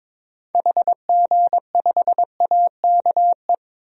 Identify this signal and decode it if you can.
Morse code, heard as a keyed tone, about 22 wpm, 700 Hz. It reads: HG5AKE
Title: DXCC-100 Call Signs - 6 Characters - 22wpm